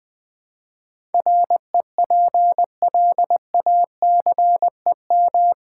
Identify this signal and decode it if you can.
Morse code, heard as a keyed tone, about 20 wpm, 700 Hz. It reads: REPLACEM
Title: Build A Word - 4 Letters At A Time - 11-Letter Words - 20wpm